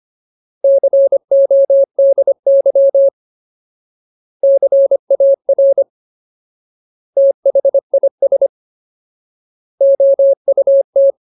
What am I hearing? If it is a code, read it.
Morse code, heard as a keyed tone, about 25 wpm, 550 Hz. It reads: CODY CAR THIS OUT
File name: Ultimate Word List / 25WPM Ultimate